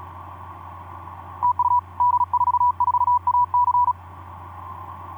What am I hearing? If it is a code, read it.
Morse code, heard as a keyed tone, about 37 wpm, 980 Hz. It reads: TU D44AC
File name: d44ac ek2rgt etc 10m cw 23nov2010 1202